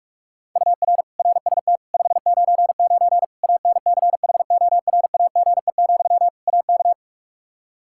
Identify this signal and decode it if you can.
Morse code, heard as a keyed tone, about 46 wpm, 700 Hz. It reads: UR RST 599 ANCHORAGE, AK
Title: K1USN SST QSO Practice 45wpm